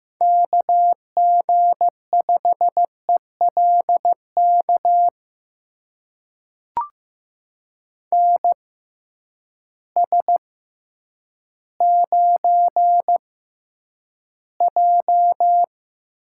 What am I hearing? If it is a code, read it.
Morse code, heard as a keyed tone, about 15 wpm, spaced as for 9 wpm, 700 Hz. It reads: KG5ELK NS9J
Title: USA Call Signs - 6x Character Spacing - 15wpm